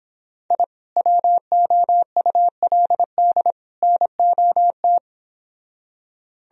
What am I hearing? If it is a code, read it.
Morse code, heard as a keyed tone, about 26 wpm, 700 Hz. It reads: I WOULD NOT
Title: Speed-Racing - Sets of 3 Words 22wpm